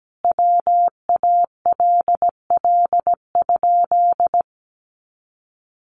Morse code, heard as a keyed tone, about 17 wpm, 700 Hz. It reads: WALL?
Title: Sentences from Top 1100 Words 17wpm